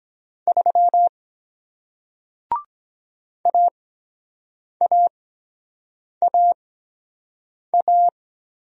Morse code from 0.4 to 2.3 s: 3

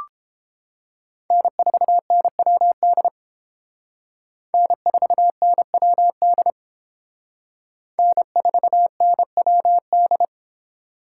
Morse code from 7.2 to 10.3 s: N4NWD

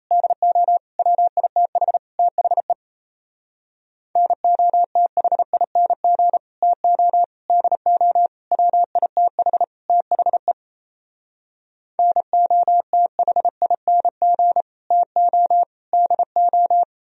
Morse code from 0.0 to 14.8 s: DO WITH THE NOTHING TO DO WITH THE NOTHING

